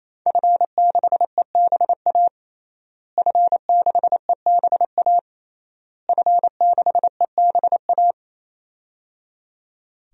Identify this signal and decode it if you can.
Morse code, heard as a keyed tone, about 28 wpm, 700 Hz. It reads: F6EBA F6EBA F6EBA
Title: DXCC-100 Call Signs - 5 Characters - Repeated 3x - 28wpm